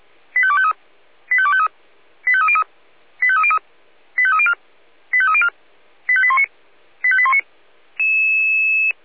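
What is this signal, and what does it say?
Five-tone selective call (ZVEI), heard as quick runs of five tones, tones of about 70 ms. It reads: 85253 85253 85303 85303 85304 85304 85819 85819
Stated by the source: ZVEI five-tone selective call 85102, received by SDR.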